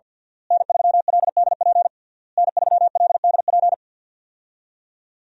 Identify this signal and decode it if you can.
Morse code, heard as a keyed tone, about 52 wpm, 700 Hz. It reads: N3LDP N3LDP
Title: USA Call Signs - 5 Characters - Repeated 3x - 50wpm